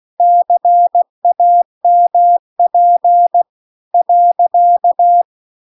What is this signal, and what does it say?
Morse code, heard as a keyed tone, about 16 wpm, 700 Hz. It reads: CAMP .